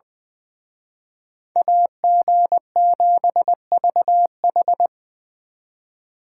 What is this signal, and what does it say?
Morse code, heard as a keyed tone, about 23 wpm, 700 Hz. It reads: AG7VH
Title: Speed Racing - USA 2x2 Call Signs 20wpm